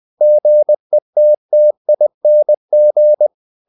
Morse code, heard as a keyed tone, about 20 wpm, 600 Hz. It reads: GETTING
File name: the old man and the sea / the old man and the sea part 1 20wpm